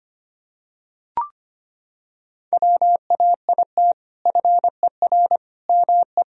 Morse code, heard as a keyed tone, about 25 wpm, 700 Hz. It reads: WAIT FER ME
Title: Sentences from Top 500 Words and Abbreviations 25wpm